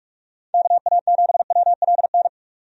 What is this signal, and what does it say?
Morse code, heard as a keyed tone, about 46 wpm, 700 Hz. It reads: KA7WLN
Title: USA Call Signs - 6 Characters - 45wpm